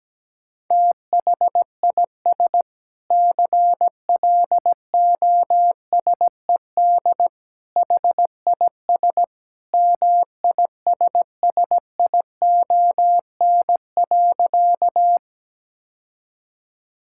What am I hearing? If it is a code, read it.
Morse code, heard as a keyed tone, about 17 wpm, 700 Hz. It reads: THIS CLOSED HIS MISSION.
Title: Sentences from Top 1600 Words 17wpm